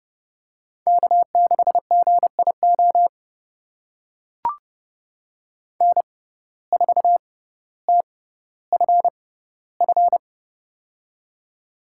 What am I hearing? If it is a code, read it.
Morse code, heard as a keyed tone, about 30 wpm, spaced as for 15 wpm, 700 Hz. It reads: K6GIO N4TFF
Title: USA Call Signs - 5 Characters - 6x Character Spacing - 30wpm